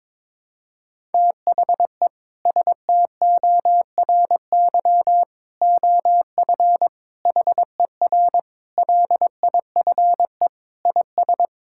Morse code, heard as a keyed tone, about 22 wpm, 700 Hz. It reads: THE STORY OF HER LIFE IS
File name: practice / 22wpm / Sentences with 3-Word Phrases - 22wpm